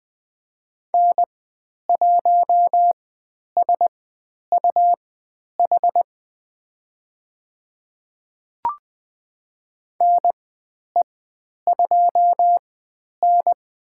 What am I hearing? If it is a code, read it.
Morse code, heard as a keyed tone, about 20 wpm, spaced as for 10 wpm, 700 Hz. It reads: N1SUH NE2N
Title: 102 Lesson 17g - Letter u - Call Signs 20wpm